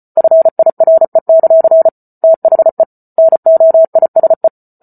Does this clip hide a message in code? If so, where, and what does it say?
Yes: Morse code from 0.2 to 4.5 s, heard as a keyed tone, about 34 wpm, 650 Hz. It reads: FIRE; THE NOISE